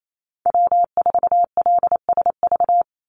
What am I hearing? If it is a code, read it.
Morse code, heard as a keyed tone, about 28 wpm, 700 Hz. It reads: W4LSV
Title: Call Signs Repeated 2x - 28wpm